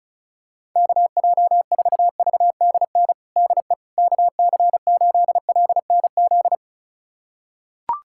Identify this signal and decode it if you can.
Morse code, heard as a keyed tone, about 34 wpm, 700 Hz. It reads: KJ4VDN DE KC8LNZ
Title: USA Callsign DE Callsign - 6 Characters - 35wpm